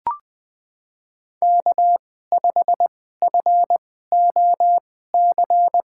Morse code, heard as a keyed tone, about 20 wpm, spaced as for 15 wpm, 700 Hz. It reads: K5FOC
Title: USA Call Signs - 5 Characters - 2x Character Spacing - 20wpm